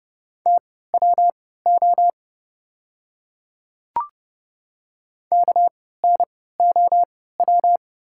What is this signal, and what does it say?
Morse code, heard as a keyed tone, about 30 wpm, spaced as for 17 wpm, 700 Hz. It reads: TWO KNOW
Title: 3x Character Spacing - Top 300 Words - 30wpm